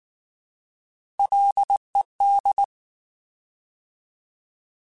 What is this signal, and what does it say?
Morse code, heard as a keyed tone, about 19 wpm, 800 Hz. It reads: LED
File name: Alphabetical / Alphabetical mp3s / Ls @18